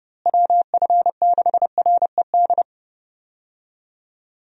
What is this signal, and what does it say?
Morse code, heard as a keyed tone, about 29 wpm, 700 Hz. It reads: WF6RED
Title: USA Call Signs - 6 Characters - Repeated 3x - 30wpm